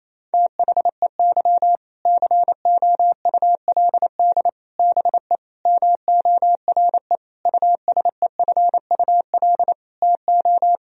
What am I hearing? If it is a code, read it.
Morse code, heard as a keyed tone, about 28 wpm, 700 Hz. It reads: THEY COULD BE MORE USEFUL TO
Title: Sentences from Top 1000 Words - Review Part 4 - 28wpm